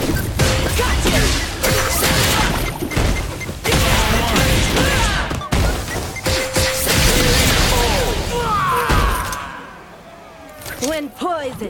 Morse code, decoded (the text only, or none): none